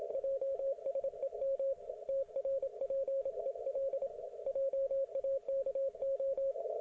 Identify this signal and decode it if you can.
Morse code, heard as a keyed tone, about 26 wpm, 540 Hz, with radio noise. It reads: JSEM TRPEL JAKO